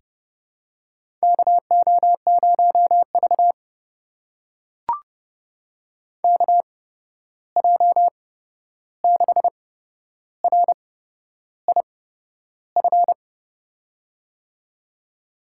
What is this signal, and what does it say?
Morse code, heard as a keyed tone, about 30 wpm, spaced as for 11 wpm, 700 Hz. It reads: KO0V KJ6RIF
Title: USA Call Signs - 8x Character Spacing - 30wpm